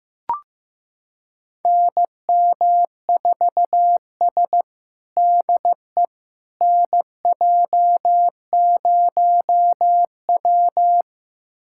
Morse code, heard as a keyed tone, about 15 wpm, 700 Hz. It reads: NM4S DE NJ0W
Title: USA Callsign DE Callsign - 4 Characters - 15wpm